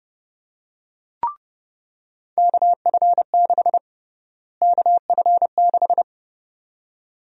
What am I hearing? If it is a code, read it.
Morse code, heard as a keyed tone, about 30 wpm, 700 Hz. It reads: KF6 KF6
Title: USA Call Sign Prefixes - Repeated 2x - 30wpm